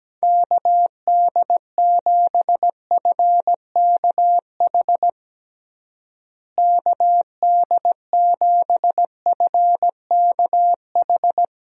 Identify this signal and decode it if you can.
Morse code, heard as a keyed tone, about 17 wpm, 700 Hz. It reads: KD7FKH KD7FKH
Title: USA Call Signs - 6 Characters - Repeated 2x - 17wpm